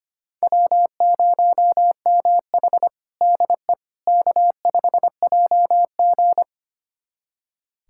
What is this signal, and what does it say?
Morse code, heard as a keyed tone, about 25 wpm, 700 Hz. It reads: W0MH DE K5JG
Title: USA Callsign DE Callsign - 4 Characters - 25wpm